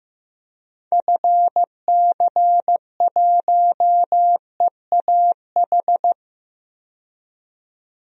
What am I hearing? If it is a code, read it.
Morse code, heard as a keyed tone, about 16 wpm, 700 Hz. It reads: FC1EAH
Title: DXCC-100 Call Signs - 6 Characters - 15wpm